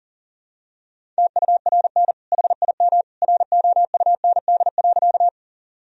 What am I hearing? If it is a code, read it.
Morse code, heard as a keyed tone, about 40 wpm, 700 Hz. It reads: TURN HIM ROUND.
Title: Sentences from Top 1000 Words - Review Part 1 - 40wpm